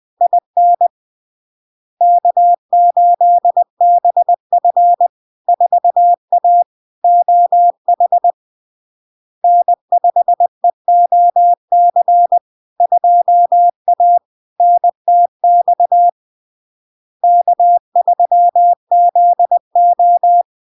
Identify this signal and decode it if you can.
Morse code, heard as a keyed tone, about 20 wpm, 700 Hz. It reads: IN K8BF 4A OH N5EOC 2A NTX K3ZO
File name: LARK-FD-2022-Log-in-CW-20wpm-700-Hz